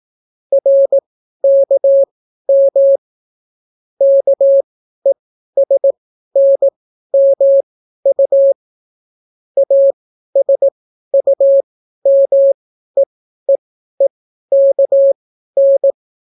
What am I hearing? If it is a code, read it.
Morse code, heard as a keyed tone, about 18 wpm, spaced as for 12 wpm, 550 Hz. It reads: RKM KESNMU ASUMEEEKN